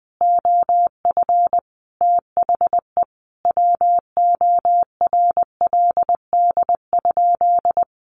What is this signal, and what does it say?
Morse code, heard as a keyed tone, about 20 wpm, 700 Hz. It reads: OF THE WORLD?
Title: Sentences from Top 900 Words 20wpm